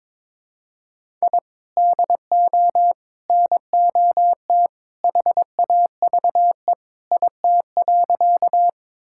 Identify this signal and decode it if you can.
Morse code, heard as a keyed tone, about 22 wpm, 700 Hz. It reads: I DO NOT HAVE IT.
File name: practice / 22wpm / Sentences with 3-Word Phrases - 22wpm